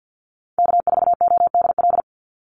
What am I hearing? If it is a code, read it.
Morse code, heard as a keyed tone, about 50 wpm, 700 Hz. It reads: K4ODL
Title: Call Signs Repeated 3x - 50wpm